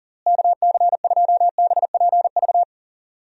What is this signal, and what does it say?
Morse code, heard as a keyed tone, about 40 wpm, 700 Hz. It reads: KC2BPV
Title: USA Call Signs - 6 Characters - Repeated 2x - 40wpm